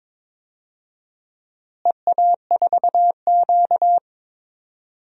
Morse code, heard as a keyed tone, about 22 wpm, 700 Hz. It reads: EA4Q
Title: DXCC-100 Call Signs - 4 Characters - Repeated 2x - 22wpm